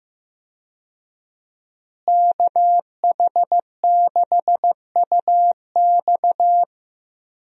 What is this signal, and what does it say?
Morse code, heard as a keyed tone, about 15 wpm, 700 Hz. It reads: KH6UX
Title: USA Call Signs - 5 Characters - Repeated 3x - 15wpm